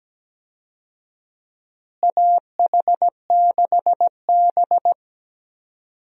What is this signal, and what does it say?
Morse code, heard as a keyed tone, about 17 wpm, 700 Hz. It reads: AH6B
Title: USA Call Signs - 4 Characters - Repeated 2x - 17wpm